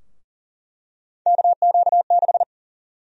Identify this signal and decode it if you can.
Morse code, heard as a keyed tone, about 40 wpm, 700 Hz. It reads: KQ6
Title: USA Call Sign Prefixes - 40wpm